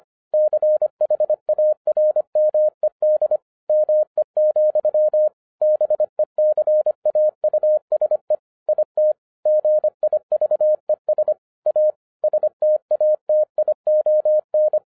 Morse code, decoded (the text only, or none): CHARMED ME, BECAUSE IT GIVES A STATION